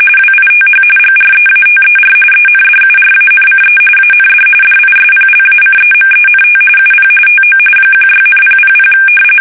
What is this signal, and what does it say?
RTTY, about 45.45 baud. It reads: WITH GREATER ENDURANCE AND DISPLACEMENT UP TO 16,000
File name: fsk850 / battleshipnj